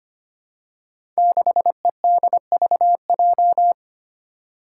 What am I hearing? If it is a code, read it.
Morse code, heard as a keyed tone, about 25 wpm, 700 Hz. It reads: 6EDVJ
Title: Code Groups - 5 Letter:Numbers - 25wpm